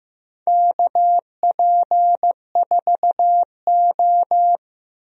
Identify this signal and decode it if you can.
Morse code, heard as a keyed tone, about 15 wpm, 700 Hz. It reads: KP4O
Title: DXCC-100 Call Signs - 4 Characters - 15wpm